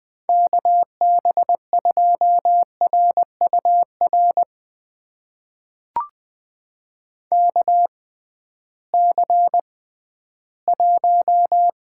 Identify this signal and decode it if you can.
Morse code, heard as a keyed tone, about 20 wpm, spaced as for 13 wpm, 700 Hz. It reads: KB2RUR KC1
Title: USA Call Signs - 6 Characters - 6x Character Spacing - 20wpm